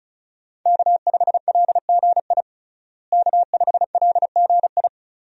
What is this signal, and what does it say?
Morse code, heard as a keyed tone, about 34 wpm, 700 Hz. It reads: K5LGI K5LGI
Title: USA Call Signs - 5 Characters - Repeated 3x - 35wpm